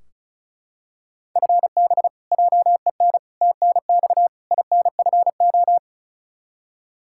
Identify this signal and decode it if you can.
Morse code, heard as a keyed tone, about 36 wpm, 700 Hz. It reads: FB JEN TNX INFO
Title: Practice QSO - Exchange 3 - More Info - 35wpm